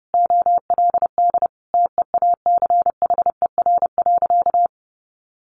Morse code, heard as a keyed tone, about 30 wpm, 700 Hz. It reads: OLD TEACHER.
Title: Sentences from Top 900 Words 30wpm